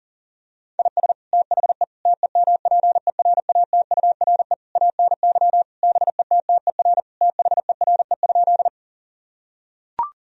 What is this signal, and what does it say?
Morse code, heard as a keyed tone, about 40 wpm, 700 Hz. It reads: IS THE TEMPERATURE ANY BETTER THERE?
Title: Sentences from Top 1000 Words - Review Part 4 - 40wpm